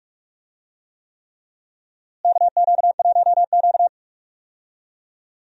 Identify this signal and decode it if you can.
Morse code, heard as a keyed tone, about 46 wpm, 700 Hz. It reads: KQ1Q
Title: USA Call Signs - 4 Characters - Repeated 3x - 45wpm